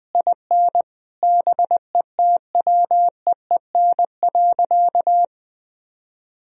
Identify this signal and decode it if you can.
Morse code, heard as a keyed tone, about 20 wpm, 700 Hz. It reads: IN BETWEEN.
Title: Sentences from Top 1000 Words - Review Part 1 - 20wpm